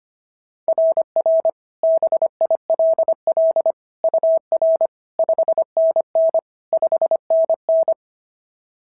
Morse code, heard as a keyed tone, about 25 wpm, 690 Hz. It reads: RR BILL UR 5NN 5NN
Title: SOTA QSO Practice 25wpm